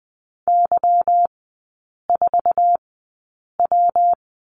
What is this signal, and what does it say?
Morse code, heard as a keyed tone, about 20 wpm, spaced as for 8 wpm, 690 Hz. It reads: Y4W
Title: 212 Lesson 34e - Number 8 - Three Characters 20wpm